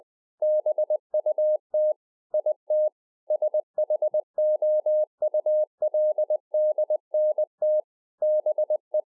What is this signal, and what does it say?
Morse code, heard as a keyed tone, about 20 wpm, 610 Hz. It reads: BUT IT SHOULDNT BE
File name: 2011-04-mp3 / A SMALL TRANCEIVER 3 20WPM